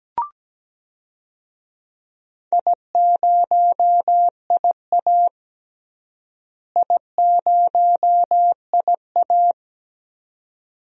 Morse code, heard as a keyed tone, about 17 wpm, 700 Hz. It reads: I0IA I0IA
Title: DXCC-100 Call Signs - 4 Characters - Repeated 2x - 17wpm